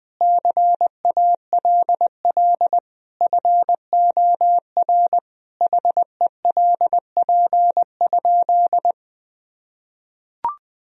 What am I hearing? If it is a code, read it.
Morse code, heard as a keyed tone, about 20 wpm, 700 Hz. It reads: CALL FOR HELP?
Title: Sentences from Top 1000 Words - Review Part 4 - 20wpm